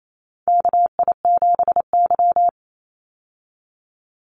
Morse code, heard as a keyed tone, about 28 wpm, 700 Hz. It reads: KI7Y
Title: Call Signs Repeated 2x - 28wpm